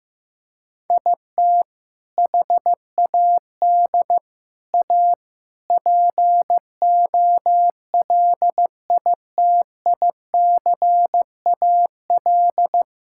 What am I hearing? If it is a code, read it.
Morse code, heard as a keyed tone, about 16 wpm, 700 Hz. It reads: IT HAD A POLITICAL
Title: Sentences from Top 1000 Words - Review Part 5 - 15wpm